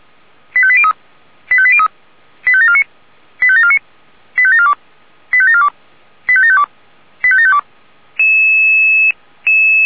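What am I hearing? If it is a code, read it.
Five-tone selective call (ZVEI), heard as quick runs of five tones, tones of about 70 ms. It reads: 85803 85803 85749 85749 85742 85742 85752 85752
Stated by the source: ZVEI five-tone selective call received on an SDR.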